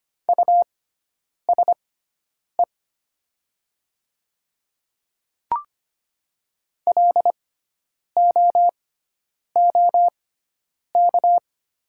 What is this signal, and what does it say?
Morse code, heard as a keyed tone, about 25 wpm, spaced as for 9 wpm, 700 Hz. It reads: USE LOOK
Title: 6x Character Spacing - Top 100 Words - 25wpm